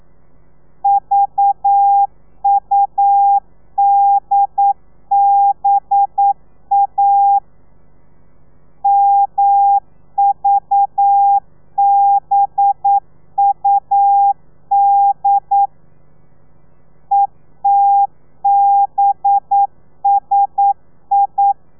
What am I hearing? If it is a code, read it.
Morse code, heard as a keyed tone, about 9 wpm, 800 Hz. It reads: VUDBA MVBUD ETBSI
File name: telegrafia / i2tfj / Lez4 40